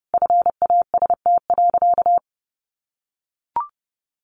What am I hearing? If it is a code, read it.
Morse code, heard as a keyed tone, about 30 wpm, 700 Hz. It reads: FAST.
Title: Sentences from Top 1300 Words 30wpm